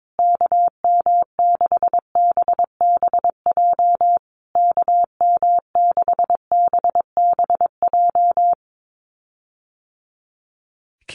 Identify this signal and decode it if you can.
Morse code, heard as a keyed tone, about 22 wpm, 700 Hz. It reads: KM6BBJ KM6BBJ
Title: USA Call Signs Repeated 3x 22wpm